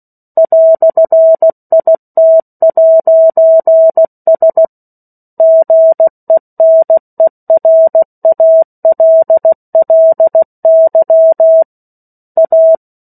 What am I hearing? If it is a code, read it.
Morse code, heard as a keyed tone, about 16 wpm, 650 Hz. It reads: "IT'S GENERALLY A
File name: 03-Chapter-03 16 WPM